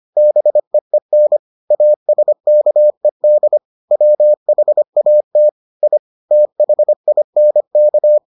Morse code, heard as a keyed tone, about 25 wpm, 600 Hz. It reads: BEEN ASKED WHAT I THINK